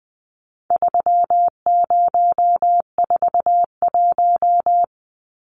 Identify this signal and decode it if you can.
Morse code, heard as a keyed tone, about 20 wpm, 700 Hz. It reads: 3041
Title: Numbers - 4-Digit 20wpm